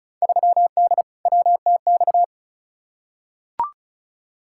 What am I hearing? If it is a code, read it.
Morse code, heard as a keyed tone, about 36 wpm, 700 Hz. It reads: 3D WTX
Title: Field Day Exchanges Encore - 35wpm